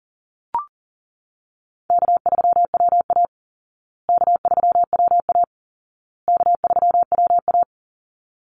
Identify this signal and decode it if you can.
Morse code, heard as a keyed tone, about 40 wpm, 700 Hz. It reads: K3WA K3WA K3WA
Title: Easy Call Signs Repeated 3x - 40wpm